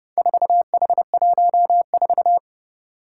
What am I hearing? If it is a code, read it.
Morse code, heard as a keyed tone, about 30 wpm, 700 Hz. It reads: 4H14
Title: Four Challenging Letter-Number Combinations 30wpm